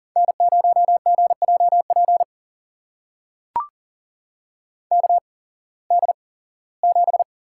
Morse code, heard as a keyed tone, about 40 wpm, spaced as for 21 wpm, 700 Hz. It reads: N0GJP KD7
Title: USA Call Signs - 8x Character Spacing - 40wpm